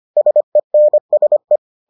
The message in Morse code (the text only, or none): SENSE